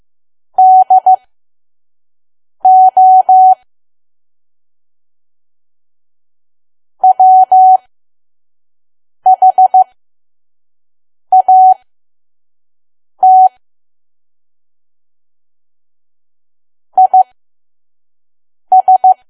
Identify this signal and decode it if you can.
Morse code, heard as a keyed tone, about 15 wpm, spaced as for 5 wpm, 750 Hz. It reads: DO WHAT IS
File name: Archive / 5 WPM files / PM